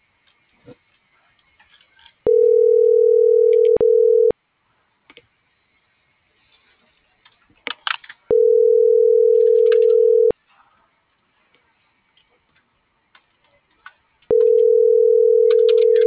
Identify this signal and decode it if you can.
Ringback tone (North America), 440 + 480 Hz.